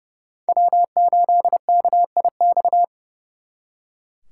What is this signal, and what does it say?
Morse code, heard as a keyed tone, about 36 wpm, 700 Hz. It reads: W8KIX
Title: Speed Racing - USA 1x3 Call Signs 30wpm